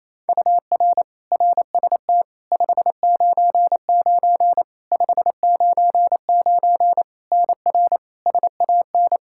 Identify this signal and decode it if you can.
Morse code, heard as a keyed tone, about 28 wpm, 700 Hz. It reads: UR RST 599 599 NR SAN
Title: Practice QSO - Exchange 1 - Beginning - 28wpm